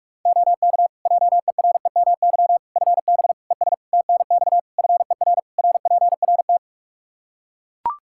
Morse code, heard as a keyed tone, about 41 wpm, 700 Hz. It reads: OK JEREMY FB ES TNX FER RPRT
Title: Practice QSO - Exchange 2 - Info - 45wpm